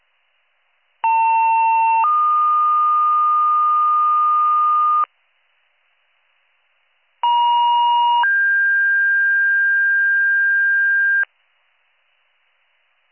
Two-tone page, heard as a one-second tone, then a three-second tone, about 900 then 1220 Hz, about 950 then 1600 Hz. It is heard from 1.0 to 11.2 s.